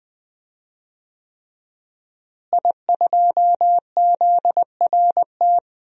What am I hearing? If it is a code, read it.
Morse code, heard as a keyed tone, about 20 wpm, 700 Hz. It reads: I2ZRT
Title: DXCC-100 Call Signs - 5 Characters - Repeated 3x - 20wpm